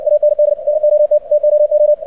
Morse code, heard as a keyed tone, about 60 wpm, 600 Hz. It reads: AND NOT TOO
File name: archived / CRSnet